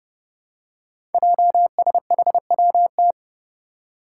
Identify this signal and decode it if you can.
Morse code, heard as a keyed tone, about 28 wpm, 700 Hz. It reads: JSHWT
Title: Five Challenging Letter-Number Combinations 30wpm